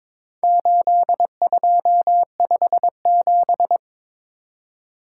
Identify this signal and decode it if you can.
Morse code, heard as a keyed tone, about 22 wpm, 700 Hz. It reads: 8257
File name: practice / 22wpm / Four Letter-Number Combinations - 22wpm